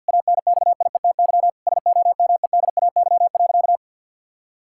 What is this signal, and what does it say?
Morse code, heard as a keyed tone, about 50 wpm, 700 Hz. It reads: ANXIETY SOMEDAY.
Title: Sentences from Top 1600 Words 50wpm